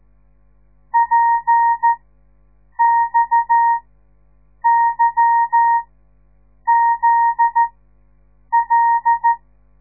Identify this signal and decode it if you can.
Morse code, heard as a keyed tone, about 13 wpm, spaced as for 7 wpm, 930 Hz. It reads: PXYZL
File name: radiomorningstar / morse / les 9